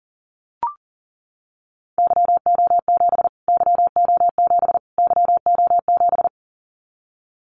Morse code, heard as a keyed tone, about 37 wpm, 700 Hz. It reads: YO7 YO7 YO7
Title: Callsign Prefixes for Top 25 DXCC - Repeated 3x - 40wpm